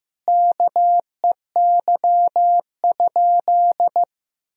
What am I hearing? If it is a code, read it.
Morse code, heard as a keyed tone, about 15 wpm, 700 Hz. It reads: KEY?
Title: Sentences from Top 1000 Words - Review Part 4 - 15wpm